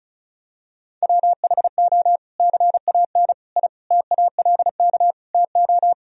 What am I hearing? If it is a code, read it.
Morse code, heard as a keyed tone, about 36 wpm, 700 Hz. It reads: WHO CAN I TALK TO
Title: Sentences from Top 1000 Words - Review Part 4 - 35wpm